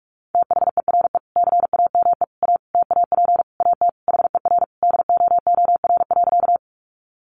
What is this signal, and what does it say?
Morse code, heard as a keyed tone, about 44 wpm, 700 Hz. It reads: THERE CAME A TAP AT HER DOOR.
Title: Sentences from Top 1400 Words 45wpm